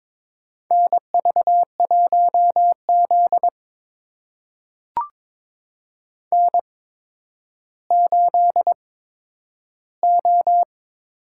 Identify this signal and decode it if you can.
Morse code, heard as a keyed tone, about 22 wpm, spaced as for 11 wpm, 700 Hz. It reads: NV1Z N8O
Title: USA Call Signs - 4 Characters - 8x Character Spacing - 22wpm